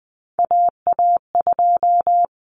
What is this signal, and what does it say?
Morse code, heard as a keyed tone, about 20 wpm, 700 Hz. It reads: AA2
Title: Common Cut Numbers - 3-Digit 20wpm